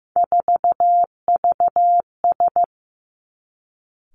Morse code, heard as a keyed tone, about 15 wpm, 700 Hz. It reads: 4VS
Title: Three Challenging Letter-Number Combinations 15wpm